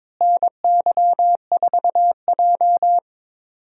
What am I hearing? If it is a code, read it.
Morse code, heard as a keyed tone, about 22 wpm, 700 Hz. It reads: NY4J
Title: USA Call Signs - 4 Characters - Repeated 3x - 22wpm